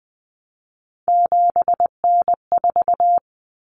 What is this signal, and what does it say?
Morse code, heard as a keyed tone, about 20 wpm, 700 Hz. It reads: 7N4
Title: Common Cut Numbers - 3-Digit 20wpm